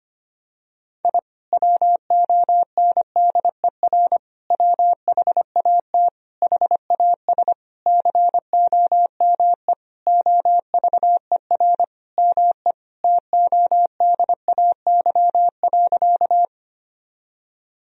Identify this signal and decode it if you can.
Morse code, heard as a keyed tone, about 25 wpm, 700 Hz. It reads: I WONDER WHAT HAS COME OVER ME TODAY.
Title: Sentences from Top 1000 Words - Review Part 4 - 25wpm